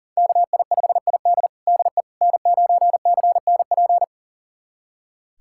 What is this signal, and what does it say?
Morse code, heard as a keyed tone, about 40 wpm, 700 Hz. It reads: KI5ID DE N9CNP
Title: USA Callsign DE Callsign - 5 Characters - 40wpm